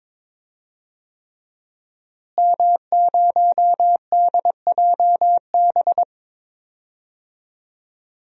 Morse code, heard as a keyed tone, about 22 wpm, 700 Hz. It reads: M0DJB
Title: DXCC-100 Call Signs - 5 Characters - Repeated 3x - 22wpm